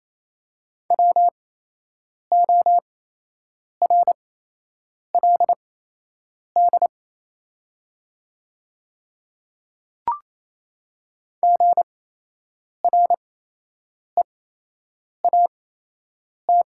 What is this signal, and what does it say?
Morse code, heard as a keyed tone, about 28 wpm, spaced as for 8 wpm, 700 Hz. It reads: WORLD GREAT
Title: 8x Character Spacing - Top 200 Words - 28wpm